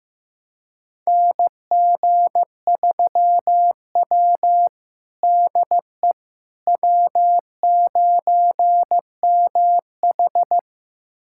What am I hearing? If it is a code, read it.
Morse code, heard as a keyed tone, about 15 wpm, 700 Hz. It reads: NG3W DE W9MH
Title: USA Callsign DE Callsign - 4 Characters - 15wpm